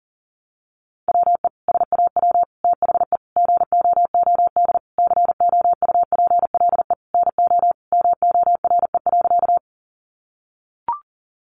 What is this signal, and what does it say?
Morse code, heard as a keyed tone, about 40 wpm, 700 Hz. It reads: WE SAW THE GOOD COUPLE NO MORE.
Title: Sentences from Top 1100 Words 40wpm